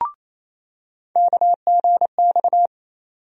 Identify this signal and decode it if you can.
Morse code, heard as a keyed tone, about 27 wpm, 700 Hz. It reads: KGX
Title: USA Call Sign Postfixes - 3 Letters 28wpm